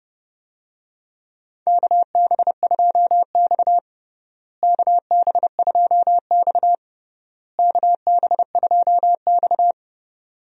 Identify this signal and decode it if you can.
Morse code, heard as a keyed tone, about 29 wpm, 700 Hz. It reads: KB2X KB2X KB2X
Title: USA Call Signs - 4 Characters - Repeated 3x - 30wpm